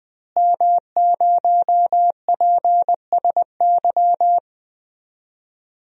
Morse code, heard as a keyed tone, about 21 wpm, 700 Hz. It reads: M0PSY
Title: DXCC-100 Call Signs - 5 Characters - Repeated 2x - 20wpm